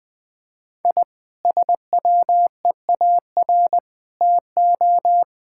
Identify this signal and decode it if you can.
Morse code, heard as a keyed tone, about 21 wpm, 700 Hz. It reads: I SWEAR TO
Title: Sentences with 9-Word Phrases - 20wpm